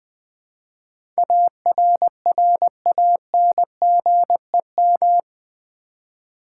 Morse code, heard as a keyed tone, about 20 wpm, 700 Hz. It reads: ARRANGEM
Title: Build A Word - 4 Letters At A Time - 11-Letter Words - 20wpm